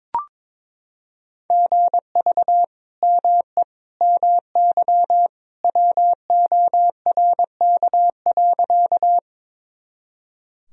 Morse code, heard as a keyed tone, about 21 wpm, 700 Hz. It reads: GV ME MY WORK.
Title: Sentences from Top 200 Words and Abbreviations 22wpm